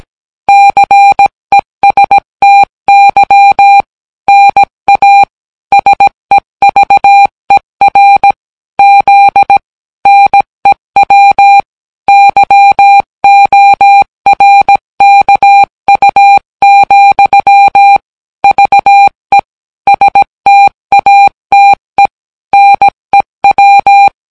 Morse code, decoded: CESTY NA SEVER Z NEW YORKU, VE STATE NEW